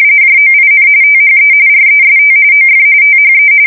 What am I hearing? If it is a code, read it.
RTTY, about 45.45 baud. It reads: JOIN THE REST OF THE